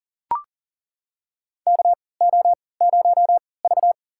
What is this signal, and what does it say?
Morse code, heard as a keyed tone, about 37 wpm, spaced as for 22 wpm, 700 Hz. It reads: KO0V
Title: USA Call Signs - 4 Characters - 3x Character Spacing - 40wpm